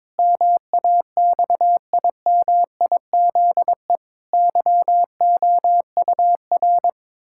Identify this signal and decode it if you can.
Morse code, heard as a keyed tone, about 22 wpm, 700 Hz. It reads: MAXIMIZE YOUR
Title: Pangram Challenge v2 - 22wpm